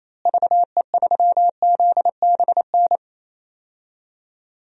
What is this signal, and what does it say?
Morse code, heard as a keyed tone, about 28 wpm, 700 Hz. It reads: VE3ZBN
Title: DXCC-100 Call Signs - 6 Characters - Repeated 2x - 28wpm